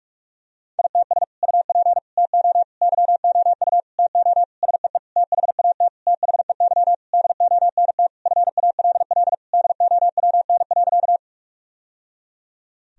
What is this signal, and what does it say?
Morse code, heard as a keyed tone, about 43 wpm, 700 Hz. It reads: ITS UP TO YOU TO SEE THAT THEY DONT FALL DOWN.
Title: Sentences from Top 1000 Words - Review Part 2 - 45wpm